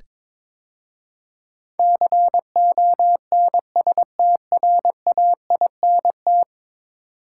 Morse code, heard as a keyed tone, about 21 wpm, 700 Hz. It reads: CONSTRAINT
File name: practice / 22wpm / 10 Letter Words - First 3 Letters Given - 22wpm